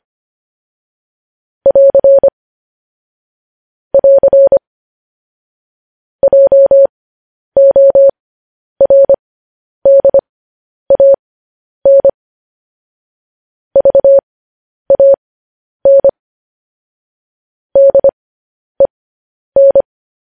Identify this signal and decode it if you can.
Morse code, heard as a keyed tone, about 25 wpm, spaced as for 10 wpm, 550 Hz. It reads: <AR> <AR> JORDAN VAN DEN